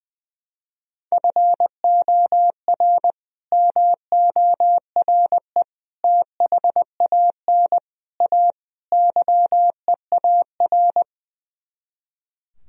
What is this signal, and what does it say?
Morse code, heard as a keyed tone, about 20 wpm, 700 Hz. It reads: FOR MORE THAN A YEAR
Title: Speed-Racing - Sets of 5 Words 20wpm